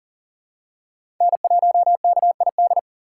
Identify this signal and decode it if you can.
Morse code, heard as a keyed tone, about 40 wpm, 700 Hz. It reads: N1KID
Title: USA Call Signs - 5 Characters - 6x Character Spacing - 40wpm